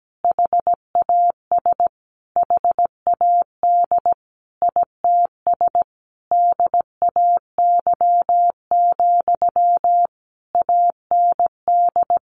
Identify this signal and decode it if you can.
Morse code, heard as a keyed tone, about 17 wpm, 700 Hz. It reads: HAS HAD ITS DAY, AND
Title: Sentences from Top 1500 Words 17wpm